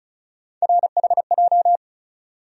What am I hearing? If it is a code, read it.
Morse code, heard as a keyed tone, about 36 wpm, 700 Hz. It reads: RHJ
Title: USA Call Sign Postfixes - 3 Letters 35wpm